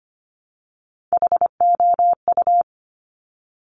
Morse code, heard as a keyed tone, about 25 wpm, 700 Hz. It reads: HOU
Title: Top Three Letter Combinations in Words 25wpm